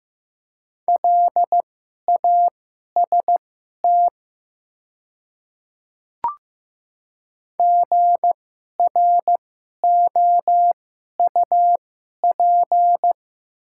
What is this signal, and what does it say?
Morse code, heard as a keyed tone, about 15 wpm, spaced as for 11 wpm, 700 Hz. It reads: LAST GROUP
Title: 2x Character Spacing - Top 300 Words - 15wpm